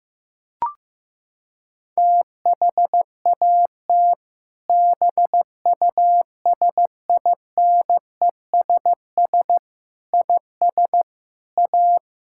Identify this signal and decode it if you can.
Morse code, heard as a keyed tone, about 15 wpm, 700 Hz. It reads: THAT BUSINESS IS A
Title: Sentences from Top 1600 Words 15wpm